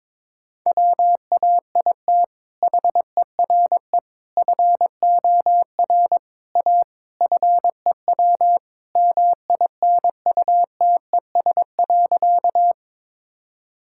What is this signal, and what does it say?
Morse code, heard as a keyed tone, about 22 wpm, 700 Hz. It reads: WAIT HERE FOR A FEW MINUTES.
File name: practice / 22wpm / Sentences with 3-Word Phrases - 22wpm